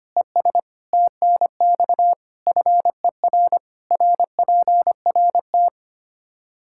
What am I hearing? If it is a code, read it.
Morse code, heard as a keyed tone, about 25 wpm, 700 Hz. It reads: ES TNX FER RPRT
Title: Practice QSO - Exchange 2 - Info - 25wpm